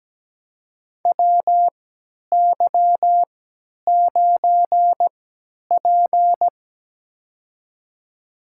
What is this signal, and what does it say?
Morse code, heard as a keyed tone, about 17 wpm, spaced as for 10 wpm, 700 Hz. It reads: WY9P
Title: USA Call Signs - 4 Characters - 3x Character Spacing - 17wpm